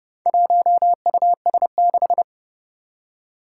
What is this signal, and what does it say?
Morse code, heard as a keyed tone, about 30 wpm, 700 Hz. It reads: 1US6
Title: Four Challenging Letter-Number Combinations 30wpm